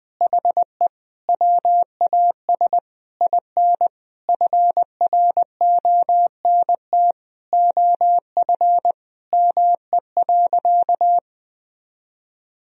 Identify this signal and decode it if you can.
Morse code, heard as a keyed tone, about 21 wpm, 700 Hz. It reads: HE WAS IN FRONT OF ME.
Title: Sentences from Top 1000 Words - Review Part 2 - 20wpm